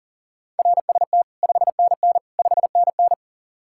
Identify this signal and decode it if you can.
Morse code, heard as a keyed tone, about 48 wpm, 700 Hz. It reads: RST 5NN 5NN
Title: Speed-Racing Common QSO Phrases - 40wpm